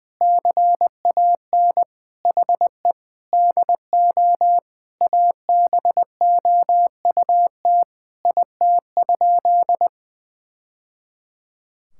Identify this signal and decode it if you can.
Morse code, heard as a keyed tone, about 20 wpm, 700 Hz. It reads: CAN HE DO ABOUT IT?
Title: Sentences from Top 1000 Words - Review Part 6 - 20wpm